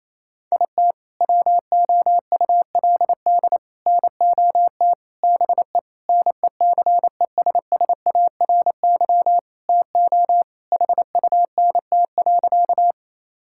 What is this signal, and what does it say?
Morse code, heard as a keyed tone, about 28 wpm, 700 Hz. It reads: IT WOULD NOT BE NECESSARY TO HUNT.
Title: Sentences from Top 1600 Words 28wpm